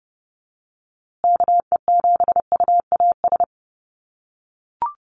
Morse code, heard as a keyed tone, about 29 wpm, 700 Hz. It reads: KE7UAS
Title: USA Call Signs Repeated 3x 30wpm